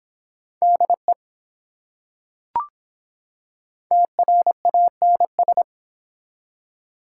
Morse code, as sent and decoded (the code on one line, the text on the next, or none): -.. / - .-. .- -. ...
DE TRANS